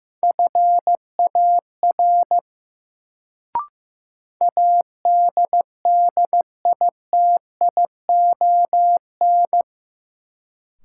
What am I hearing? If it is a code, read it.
Morse code, heard as a keyed tone, about 16 wpm, 700 Hz. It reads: FAR ADDITION
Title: Sets of 3 Words - New Word Drill - 15wpm